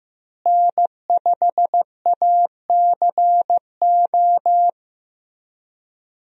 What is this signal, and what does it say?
Morse code, heard as a keyed tone, about 15 wpm, 700 Hz. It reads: N5ACO
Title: USA Call Signs - 5 Characters - Repeated 3x - 15wpm